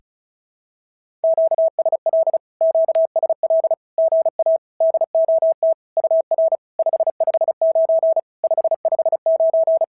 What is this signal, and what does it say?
Morse code, heard as a keyed tone, about 34 wpm, 650 Hz. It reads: QSL QSL GA DOT UR 559 559
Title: SOTA QSO Practice 35wpm